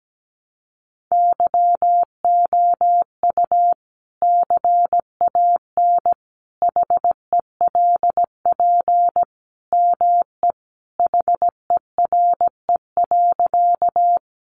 Morse code, heard as a keyed tone, about 17 wpm, 700 Hz. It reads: YOU CAN HELP ME HERE.
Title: Sentences from Top 200 Words 17wpm